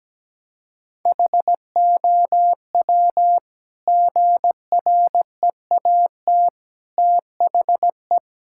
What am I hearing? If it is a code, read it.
Morse code, heard as a keyed tone, about 17 wpm, 700 Hz. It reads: HOW GREAT THE
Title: Sentences from Top 1000 Words - Review Part 5 - 17wpm